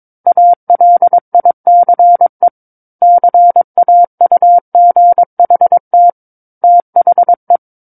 Morse code, heard as a keyed tone, about 22 wpm, 700 Hz. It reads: ALICE CAUGHT THE